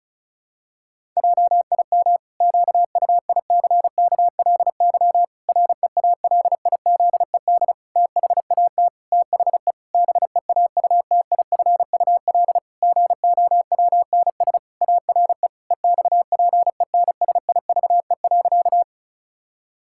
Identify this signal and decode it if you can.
Morse code, heard as a keyed tone, about 34 wpm, 700 Hz. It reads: JIM QUICKLY REALIZED THAT THE BEAUTIFUL GOWNS ARE EXPENSIVE.
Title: Pangram Challenge v2 - 35wpm